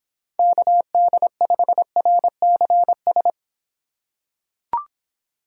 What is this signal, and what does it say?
Morse code, heard as a keyed tone, about 26 wpm, 700 Hz. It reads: KD5RCS